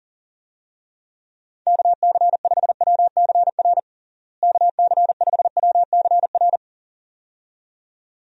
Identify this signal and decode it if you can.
Morse code, heard as a keyed tone, about 40 wpm, 700 Hz. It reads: KC5WCR KC5WCR
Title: USA Call Signs - 6 Characters - Repeated 2x - 40wpm